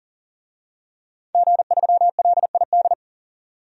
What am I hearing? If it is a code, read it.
Morse code, heard as a keyed tone, about 40 wpm, 700 Hz. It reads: G3LID
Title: DXCC-100 Call Signs - 5 Characters - 40wpm